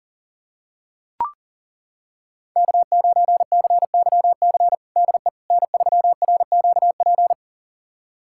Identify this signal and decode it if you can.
Morse code, heard as a keyed tone, about 37 wpm, 690 Hz. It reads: K9CYC DE N3RQP
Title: USA Callsign DE Callsign - 5 Characters - 40wpm